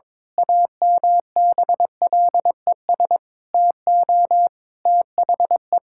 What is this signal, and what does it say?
Morse code, heard as a keyed tone, about 22 wpm, 700 Hz. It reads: AMBLES TO THE
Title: Pangram Challenge v2 - 22wpm